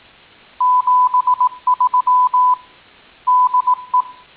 Morse code, heard as a keyed tone, about 18 wpm, 1000 Hz. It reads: 73 DE
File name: tin-upgrade / QSO 2019 08 13